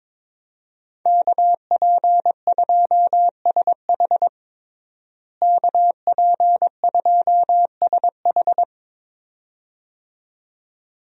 Morse code, heard as a keyed tone, about 22 wpm, 700 Hz. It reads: KP2SH KP2SH
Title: DXCC-100 Call Signs - 5 Characters - Repeated 2x - 22wpm